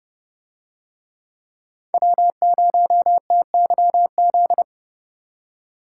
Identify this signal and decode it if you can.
Morse code, heard as a keyed tone, about 29 wpm, 700 Hz. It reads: W0TYZ